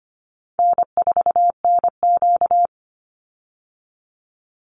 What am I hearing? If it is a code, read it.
Morse code, heard as a keyed tone, about 25 wpm, 700 Hz. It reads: N4NQ
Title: Call Signs Repeated 3x - 25wpm